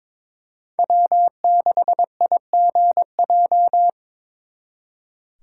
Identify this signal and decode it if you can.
Morse code, heard as a keyed tone, about 26 wpm, 700 Hz. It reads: W6IGJ
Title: Speed Racing - USA 1x3 Call Signs 22wpm